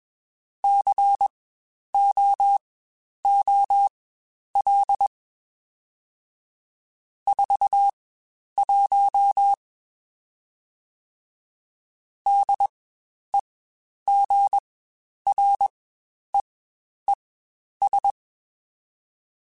Morse code, heard as a keyed tone, about 21 wpm, spaced as for 10 wpm, 800 Hz. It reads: COOL 41 DEGREES
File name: Spaced Characters / Spaced Characters-mp3s / Weather 4 14 sp @20